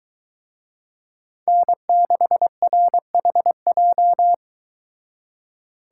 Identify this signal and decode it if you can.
Morse code, heard as a keyed tone, about 23 wpm, 700 Hz. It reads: N6RHJ